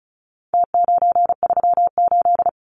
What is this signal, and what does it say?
Morse code, heard as a keyed tone, about 34 wpm, 700 Hz. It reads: T938
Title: Numbers - Contest Style - 4-Digit 35wpm